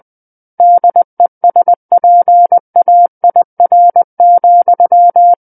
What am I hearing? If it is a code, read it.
Morse code, heard as a keyed tone, about 20 wpm, 700 Hz. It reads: DESPAIR,
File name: 20 WPM